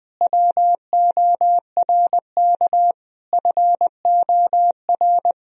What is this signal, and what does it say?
Morse code, heard as a keyed tone, about 21 wpm, 700 Hz. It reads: WORK FOR
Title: Sentences from Top 1000 Words - Review Part 2 - 20wpm